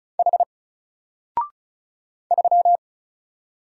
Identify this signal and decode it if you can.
Morse code, heard as a keyed tone, about 36 wpm, 700 Hz. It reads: H 3
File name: Single-Letter-Number-35wpm